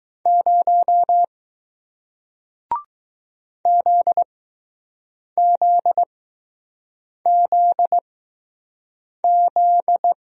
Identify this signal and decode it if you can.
Morse code, heard as a keyed tone, about 20 wpm, spaced as for 6 wpm, 700 Hz. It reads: 0 ZZZZ